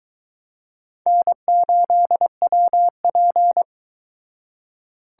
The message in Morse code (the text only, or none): N8WP